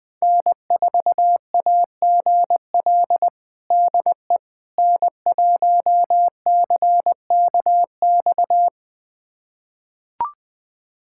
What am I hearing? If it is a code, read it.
Morse code, heard as a keyed tone, about 21 wpm, 700 Hz. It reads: N4AGL DE N1CKX
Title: USA Callsign DE Callsign - 5 Characters - 20wpm